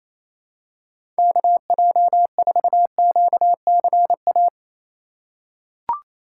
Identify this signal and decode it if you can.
Morse code, heard as a keyed tone, about 28 wpm, 700 Hz. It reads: KJ4QCA
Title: USA Call Signs - 4x Character Spacing - 28wpm